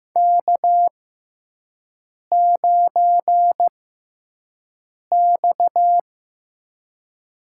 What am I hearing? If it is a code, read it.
Morse code, heard as a keyed tone, about 15 wpm, spaced as for 5 wpm, 700 Hz. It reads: K9X
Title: USA Call Signs - 4 Characters - 6x Character Spacing - 15wpm